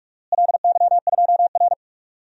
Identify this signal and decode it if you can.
Morse code, heard as a keyed tone, about 47 wpm, 700 Hz. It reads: LY2R
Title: DXCC-100 Call Signs - 4 Characters - Repeated 2x - 45wpm